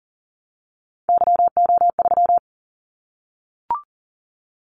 Morse code, heard as a keyed tone, about 38 wpm, 700 Hz. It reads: YO3
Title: Callsign Prefixes for Top 25 DXCC - 40wpm